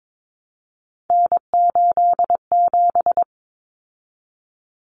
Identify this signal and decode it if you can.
Morse code, heard as a keyed tone, about 22 wpm, 700 Hz. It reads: N87
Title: Common Cut Numbers - 3-Digit 22wpm